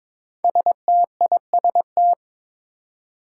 Morse code, heard as a keyed tone, about 23 wpm, 700 Hz. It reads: STIST
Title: Five Challenging Letter-Number Combinations 22wpm